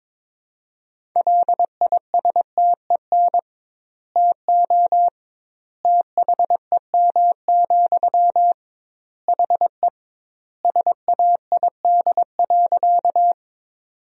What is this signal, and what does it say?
Morse code, heard as a keyed tone, about 22 wpm, 700 Hz. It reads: LISTEN TO THEM, HE SAID.
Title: 2x Word-Spacing - Sentences from Top 400 Words - 22wpm